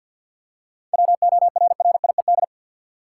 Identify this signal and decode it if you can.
Morse code, heard as a keyed tone, about 53 wpm, 700 Hz. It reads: WORRIED
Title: Seven Letter Words - 50wpm